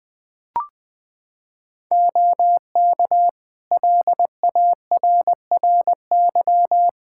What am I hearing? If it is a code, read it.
Morse code, heard as a keyed tone, about 20 wpm, 700 Hz. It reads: OK LARRY
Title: Practice QSO - Exchange 3 - More Info - 20wpm